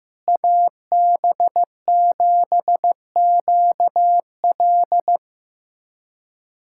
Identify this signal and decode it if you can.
Morse code, heard as a keyed tone, about 15 wpm, 700 Hz. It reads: AB7QL
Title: USA Callsign DE Callsign - 5 Characters - 15wpm